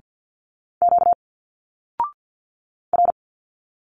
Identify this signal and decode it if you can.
Morse code, heard as a keyed tone, about 50 wpm, 700 Hz. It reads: Q R
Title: Single Letters 50wpm